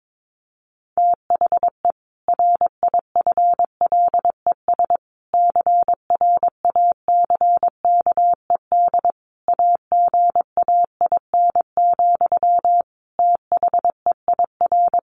Morse code, heard as a keyed tone, about 22 wpm, 700 Hz. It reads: THE RIFLES CRACKED AGAIN, THEIR
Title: A Sound Of Thunder 22wpm